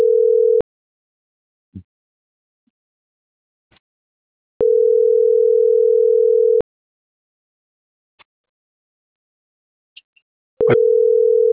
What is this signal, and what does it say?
Ringback tone (North America), 440 + 480 Hz.